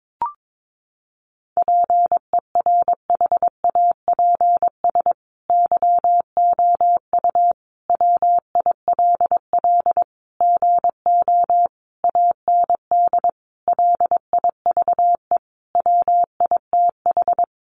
Morse code, heard as a keyed tone, about 22 wpm, 700 Hz. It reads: PERHAPS YOU WILL GO AND LIVE WITH
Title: Sentences from Top 500 Words 22wpm